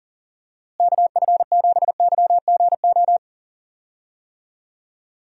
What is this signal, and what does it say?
Morse code, heard as a keyed tone, about 40 wpm, 700 Hz. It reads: KF7YGO
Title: USA Call Signs - 6 Characters - Repeated 2x - 40wpm